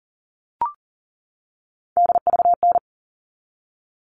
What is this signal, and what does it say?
Morse code, heard as a keyed tone, about 40 wpm, 700 Hz. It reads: DVN
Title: All Cut Numbers - 3-Digit 40wpm